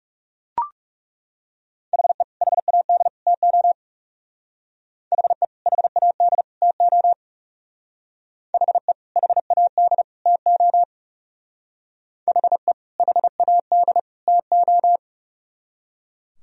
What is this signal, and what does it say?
Morse code, heard as a keyed tone, about 35 wpm, 700 Hz. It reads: HE HAD TO HE HAD TO HE HAD TO HE HAD TO
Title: Speed-Racing - Sets of 3 Words 30wpm